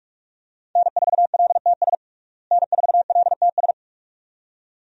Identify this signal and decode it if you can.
Morse code, heard as a keyed tone, about 47 wpm, 700 Hz. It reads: N4LTS N4LTS
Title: USA Call Signs - 5 Characters - Repeated 2x - 45wpm